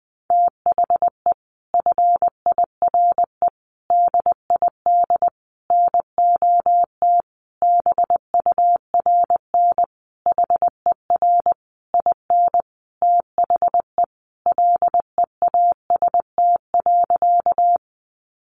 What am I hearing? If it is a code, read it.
Morse code, heard as a keyed tone, about 20 wpm, 700 Hz. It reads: THE FIRE DID NOT BURN HER IN THE LEAST.
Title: Sentences from Top 1300 Words 20wpm